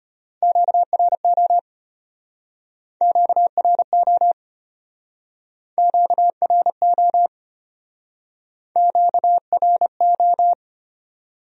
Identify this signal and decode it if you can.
Morse code, heard as a keyed tone, about 34 wpm, 700 Hz. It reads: QRO QRO QRO QRO